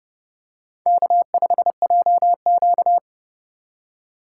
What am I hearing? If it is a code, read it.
Morse code, heard as a keyed tone, about 30 wpm, 700 Hz. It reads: K5JQ
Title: USA Call Signs - 4 Characters - 3x Character Spacing - 30wpm